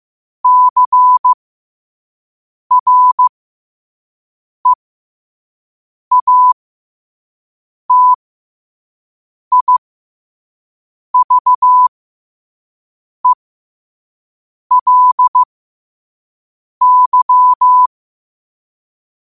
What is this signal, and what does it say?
Morse code, heard as a keyed tone, about 15 wpm, spaced as for 5 wpm, 1000 Hz. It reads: CREATIVELY
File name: qotdfiles / 5wpm